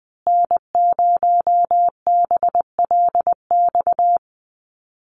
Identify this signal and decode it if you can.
Morse code, heard as a keyed tone, about 20 wpm, 700 Hz. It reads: N0BLX
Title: USA Call Signs Repeated 2x 20wpm